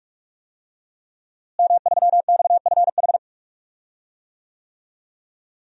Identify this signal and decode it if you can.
Morse code, heard as a keyed tone, about 47 wpm, 700 Hz. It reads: M3XFH
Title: DXCC-100 Call Signs - 5 Characters - Repeated 2x - 45wpm